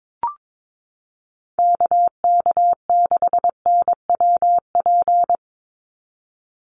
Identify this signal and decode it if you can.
Morse code, heard as a keyed tone, about 22 wpm, 700 Hz. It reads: KK6NWP
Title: USA Call Signs 22wpm